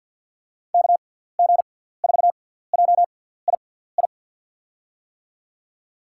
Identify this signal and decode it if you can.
Morse code, heard as a keyed tone, about 47 wpm, spaced as for 17 wpm, 700 Hz. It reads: KG4JII
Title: USA Call Signs - 6x Character Spacing - 50wpm